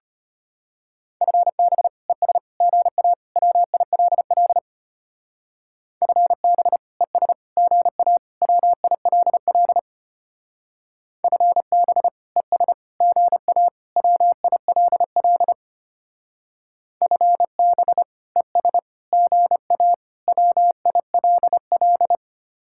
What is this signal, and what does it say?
Morse code, heard as a keyed tone, about 32 wpm, 700 Hz. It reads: FB ES GA WILL FB ES GA WILL FB ES GA WILL FB ES GA WILL